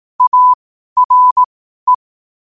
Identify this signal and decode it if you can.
Morse code, heard as a keyed tone, about 18 wpm, spaced as for 12 wpm, 1000 Hz. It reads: ARE
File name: qotdfiles / 12wpm